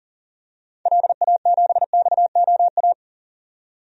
Morse code, heard as a keyed tone, about 40 wpm, 700 Hz. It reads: LA7XOA
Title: DXCC-100 Call Signs - 6 Characters - 40wpm